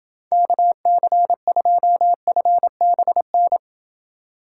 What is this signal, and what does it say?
Morse code, heard as a keyed tone, about 30 wpm, 700 Hz. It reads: KC2FBN